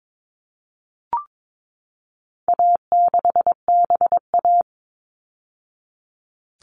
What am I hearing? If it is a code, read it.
Morse code, heard as a keyed tone, about 22 wpm, 700 Hz. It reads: A6BA
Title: All Cut Numbers - 4-Digit 22wpm